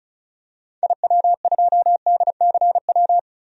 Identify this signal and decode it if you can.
Morse code, heard as a keyed tone, about 36 wpm, 700 Hz. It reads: IW2DCW
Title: DXCC-100 Call Signs - 6 Characters - 35wpm